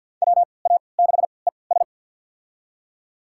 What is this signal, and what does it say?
Morse code, heard as a keyed tone, about 52 wpm, spaced as for 29 wpm, 700 Hz. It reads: WA6ES
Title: USA Call Signs - 5 Characters - 3x Character Spacing - 50wpm